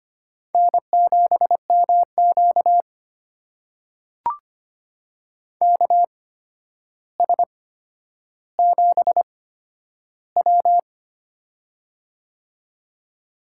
Morse code, heard as a keyed tone, about 25 wpm, spaced as for 11 wpm, 700 Hz. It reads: N7MQ KS7W
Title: USA Call Signs - 4 Characters - 8x Character Spacing - 25wpm